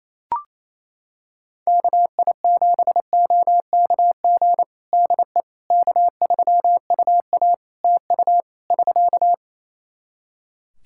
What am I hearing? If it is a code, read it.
Morse code, heard as a keyed tone, about 28 wpm, 700 Hz. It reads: KI7OKG DE K3UA TU <SK>
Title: Practice QSO - Exchange 4 - Ending - 28wpm